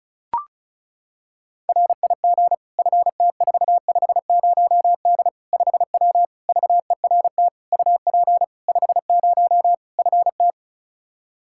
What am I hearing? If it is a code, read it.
Morse code, heard as a keyed tone, about 36 wpm, 700 Hz. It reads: RIG FT450D 5W VERT UP 50 FT
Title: Practice QSO - Exchange 2 - Info - 35wpm